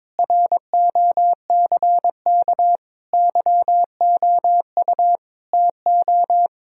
Morse code, heard as a keyed tone, about 22 wpm, 700 Hz. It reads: ROCK YOU TO
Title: Sentences from Top 1000 Words - Review Part 2 - 22wpm